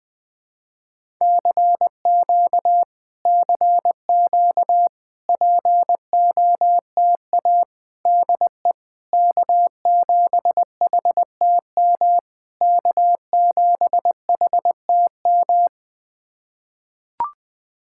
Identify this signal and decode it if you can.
Morse code, heard as a keyed tone, about 20 wpm, 700 Hz. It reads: CQ CQ POTA DE K7HTM K7HTM
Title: POTA QSO Practice - 20wpm